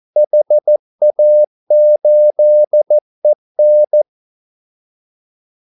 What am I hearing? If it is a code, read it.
Morse code, heard as a keyed tone, about 14 wpm, 600 Hz. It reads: HA8EN
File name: Callsigns-14wpm